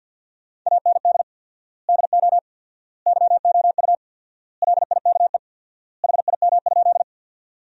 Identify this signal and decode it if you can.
Morse code, heard as a keyed tone, about 50 wpm, 700 Hz. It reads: AND DO YOU LIKE HIM?
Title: 4x Word-Spacing - Sentences from Top 100 Words - 50wpm